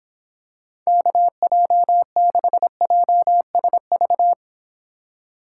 Morse code, heard as a keyed tone, about 26 wpm, 700 Hz. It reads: KJ6JSV